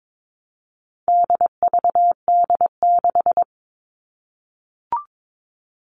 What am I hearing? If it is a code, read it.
Morse code, heard as a keyed tone, about 22 wpm, 700 Hz. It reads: DVD6
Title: All Cut Numbers - 4-Digit 22wpm